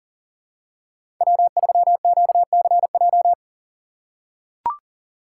Morse code, heard as a keyed tone, about 40 wpm, 700 Hz. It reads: W3QCJ
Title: USA Call Signs - 5 Characters - 4x Character Spacing - 40wpm